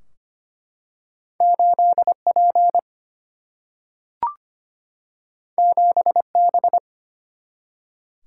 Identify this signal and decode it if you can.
Morse code, heard as a keyed tone, about 25 wpm, 700 Hz. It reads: 8P 7B
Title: Single-Letter Single-Number 25wpm